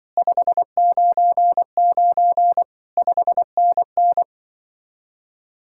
Morse code, heard as a keyed tone, about 24 wpm, 700 Hz. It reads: 599 5NN
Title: Speed-Racing Common QSO Phrases - 20wpm